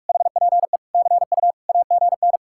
Morse code, heard as a keyed tone, about 50 wpm, 700 Hz. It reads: HPE CU AGN